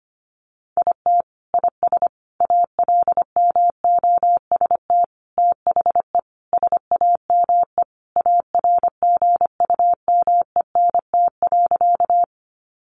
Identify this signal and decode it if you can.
Morse code, heard as a keyed tone, about 25 wpm, 700 Hz. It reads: IT IS ALMOST THE SAME ARGUMENT.
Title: Sentences from Top 1400 Words 25wpm